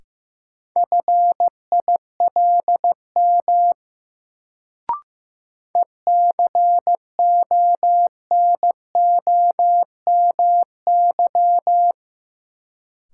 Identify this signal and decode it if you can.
Morse code, heard as a keyed tone, about 15 wpm, 700 Hz. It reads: FILM ECONOMY
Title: Sentences from Top 1000 Words - Review Part 3 - New Word Drill - 15wpm